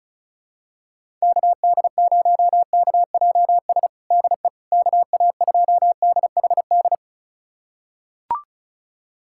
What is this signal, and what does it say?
Morse code, heard as a keyed tone, about 34 wpm, 700 Hz. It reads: KD0KJS DE KA2DHD
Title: USA Callsign DE Callsign - 6 Characters - 35wpm